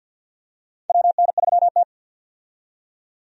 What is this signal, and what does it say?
Morse code, heard as a keyed tone, about 52 wpm, 700 Hz. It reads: WN3T
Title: USA Call Signs - 4 Characters - 50wpm